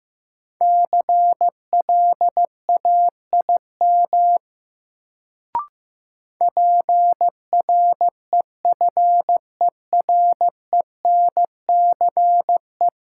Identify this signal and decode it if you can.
Morse code, heard as a keyed tone, about 15 wpm, 700 Hz. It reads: CLAIM PREFERENCE
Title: Sentences from Top 1500 Words - New Word Drill - 15wpm